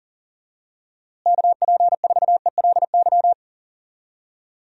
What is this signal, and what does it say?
Morse code, heard as a keyed tone, about 40 wpm, 700 Hz. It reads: KP4ELY
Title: USA Call Signs - 6 Characters - Repeated 3x - 40wpm